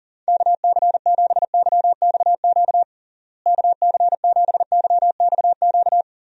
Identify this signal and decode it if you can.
Morse code, heard as a keyed tone, about 40 wpm, 700 Hz. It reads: KC7YXQ KC7YXQ
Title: USA Call Signs - 6 Characters - Repeated 3x - 40wpm